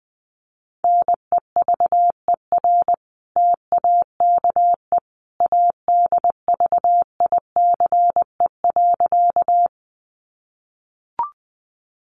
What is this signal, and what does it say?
Morse code, heard as a keyed tone, about 20 wpm, 700 Hz. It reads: NEVER TAKE ADVICE.
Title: Sentences from Top 1100 Words 20wpm